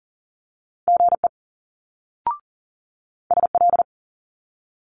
Morse code, heard as a keyed tone, about 40 wpm, 700 Hz. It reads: GE SL
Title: Multi-letter Phonemes Quick Fire 40wpm